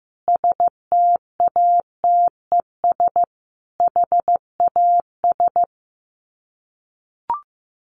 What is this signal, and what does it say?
Morse code, heard as a keyed tone, about 15 wpm, 700 Hz. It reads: STATES HAS
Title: Sets of 4 Words 15wpm